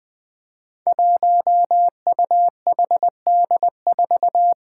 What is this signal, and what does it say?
Morse code, heard as a keyed tone, about 20 wpm, 700 Hz. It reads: 1UHD4
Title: Five Challenging Letter-Number Combinations 20wpm